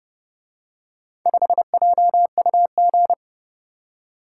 Morse code, heard as a keyed tone, about 28 wpm, 700 Hz. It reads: 5JUG